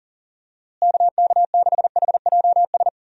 Morse code, heard as a keyed tone, about 37 wpm, 700 Hz. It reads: KK6HJS
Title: USA Call Signs - 6 Characters - Repeated 2x - 40wpm